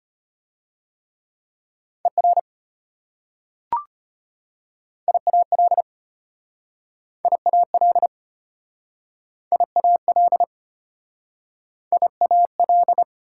Morse code, heard as a keyed tone, about 33 wpm, 700 Hz. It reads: ER IAL IAL IAL IAL